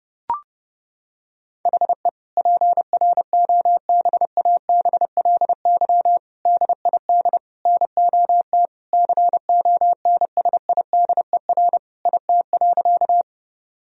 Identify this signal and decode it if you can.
Morse code, heard as a keyed tone, about 30 wpm, 700 Hz. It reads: HE PROBABLY DID NOT CONSIDER IT.
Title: Sentences from Top 1000 Words - Review Part 4 - 30wpm